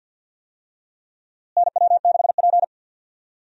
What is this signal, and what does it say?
Morse code, heard as a keyed tone, about 52 wpm, 700 Hz. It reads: NW6P
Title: USA Call Signs - 4 Characters - Repeated 3x - 50wpm